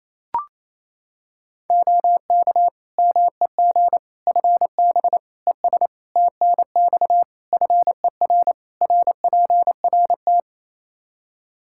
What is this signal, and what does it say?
Morse code, heard as a keyed tone, about 28 wpm, 700 Hz. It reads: OK MEG FB ES TNX FER RPRT
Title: Practice QSO - Exchange 2 - Info - 28wpm